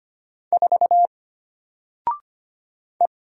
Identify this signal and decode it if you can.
Morse code, heard as a keyed tone, about 25 wpm, 700 Hz. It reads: 4 E
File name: Single-Letter-Number-25wpm